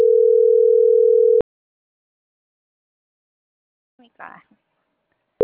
Ringback tone (North America), 440 + 480 Hz.